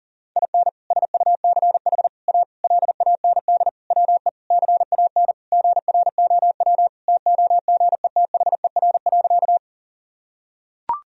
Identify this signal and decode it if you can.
Morse code, heard as a keyed tone, about 40 wpm, 700 Hz. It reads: IN SUCH A LAND WE CAN GROW TOGETHER.
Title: Sentences from Top 1000 Words - Review Part 1 - 40wpm